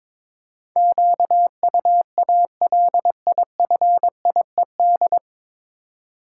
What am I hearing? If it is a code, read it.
Morse code, heard as a keyed tone, about 22 wpm, 700 Hz. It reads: QUALIFIED
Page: Morse Code Ninja practice 22wpm